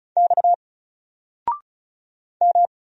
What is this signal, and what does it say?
Morse code, heard as a keyed tone, about 35 wpm, 700 Hz. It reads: X M
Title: Single Letter-Number 35wpm